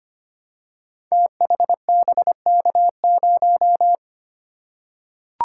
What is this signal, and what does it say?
Morse code, heard as a keyed tone, about 25 wpm, 700 Hz. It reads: THBK0
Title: Code Groups - 5 Letter:Numbers - 25wpm